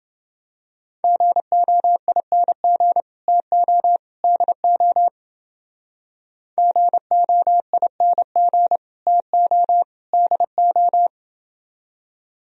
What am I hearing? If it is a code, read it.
Morse code, heard as a keyed tone, about 28 wpm, 700 Hz. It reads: GOING TO DO GOING TO DO